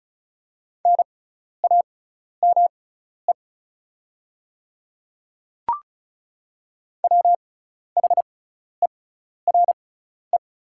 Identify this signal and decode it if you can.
Morse code, heard as a keyed tone, about 36 wpm, spaced as for 12 wpm, 700 Hz. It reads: NAME WHERE
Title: 6x Character Spacing - Top 200 Words - 35wpm